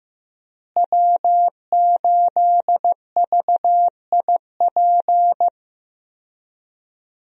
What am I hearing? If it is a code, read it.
Morse code, heard as a keyed tone, about 15 wpm, 700 Hz. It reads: W8VIP
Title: USA Call Signs - 5 Characters - Repeated 2x - 15wpm